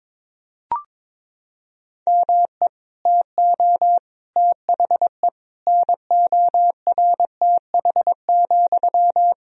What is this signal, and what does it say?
Morse code, heard as a keyed tone, about 21 wpm, 700 Hz. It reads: ME TO THE NORTH,
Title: Sentences from Top 400 Words and Abbreviations 22wpm